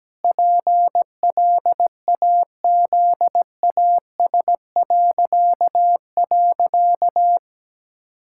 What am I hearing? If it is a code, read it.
Morse code, heard as a keyed tone, about 17 wpm, 700 Hz. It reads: PLAZAS..
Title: Pangram Challenge v2 - 17wpm